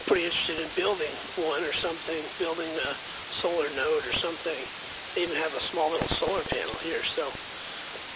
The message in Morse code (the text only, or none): none